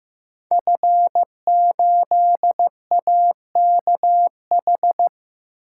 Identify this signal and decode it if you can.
Morse code, heard as a keyed tone, about 15 wpm, 700 Hz. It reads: F8AKH
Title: Code Groups - 5 Letter:Numbers - 15wpm